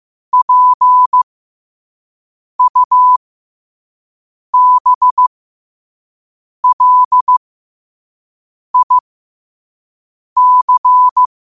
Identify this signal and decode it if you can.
Morse code, heard as a keyed tone, about 15 wpm, spaced as for 5 wpm, 1000 Hz. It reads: PUBLIC
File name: qotdfiles / 5wpm